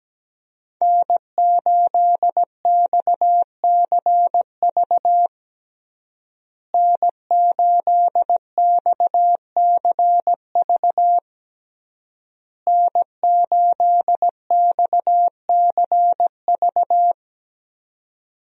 Morse code, heard as a keyed tone, about 17 wpm, 700 Hz. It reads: N8XCV N8XCV N8XCV
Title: USA Call Signs - 5 Characters - Repeated 3x - 17wpm